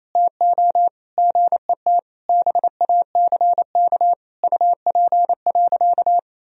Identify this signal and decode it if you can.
Morse code, heard as a keyed tone, about 28 wpm, 700 Hz. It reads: TO GET BACK UP.
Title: Sentences from Top 1000 Words - Review Part 2 - 28wpm